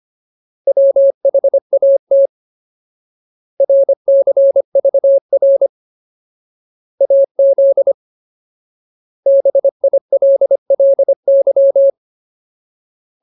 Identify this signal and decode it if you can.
Morse code, heard as a keyed tone, about 25 wpm, 550 Hz. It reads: WHAT RCVR AZ BILLY
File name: Ultimate Word List / 25WPM Ultimate